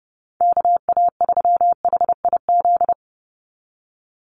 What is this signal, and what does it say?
Morse code, heard as a keyed tone, about 30 wpm, 700 Hz. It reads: KA3HIZ
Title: USA Call Signs 30wpm